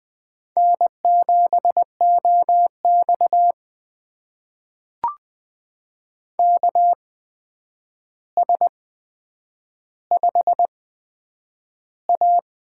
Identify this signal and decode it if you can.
Morse code, heard as a keyed tone, about 20 wpm, spaced as for 9 wpm, 700 Hz. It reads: N7OX KS5A
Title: USA Call Signs - 4 Characters - 8x Character Spacing - 20wpm